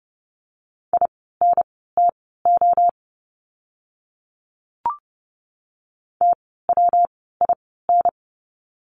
Morse code, heard as a keyed tone, about 30 wpm, spaced as for 17 wpm, 700 Hz. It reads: INTO TWIN
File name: course / 30:10wpm / 115 Lesson 19f - Letter w - Words 30wpm